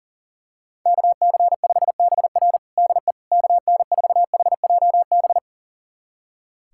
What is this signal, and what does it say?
Morse code, heard as a keyed tone, about 40 wpm, 700 Hz. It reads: KC5BR DE KN4HJB
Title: USA Callsign DE Callsign 40wpm